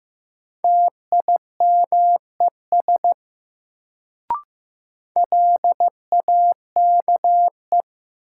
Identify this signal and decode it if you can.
Morse code, heard as a keyed tone, about 15 wpm, 700 Hz. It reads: TIMES LAKE
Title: Sentences from Top 1100 Words - New Word Drill Alt - 15wpm